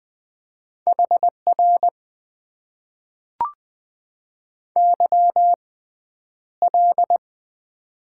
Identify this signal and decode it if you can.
Morse code, heard as a keyed tone, about 20 wpm, spaced as for 10 wpm, 700 Hz. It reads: HR YL
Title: Top 90 QSO Elements - 6x Character Spacing - 20wpm